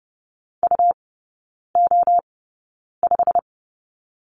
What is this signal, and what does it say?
Morse code, heard as a keyed tone, about 30 wpm, spaced as for 9 wpm, 700 Hz. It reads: UO5
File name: Lesson 17e - Letter u - Three Characters 30wpm